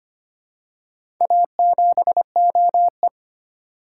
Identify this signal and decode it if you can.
Morse code, heard as a keyed tone, about 25 wpm, 700 Hz. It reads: A7OE
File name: practice / 25wpm / Four Letter-Number Combinations - 25wpm